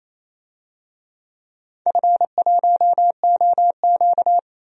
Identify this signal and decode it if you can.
Morse code, heard as a keyed tone, about 28 wpm, 700 Hz. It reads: F1OQ
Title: DXCC-100 Call Signs - 4 Characters - 28wpm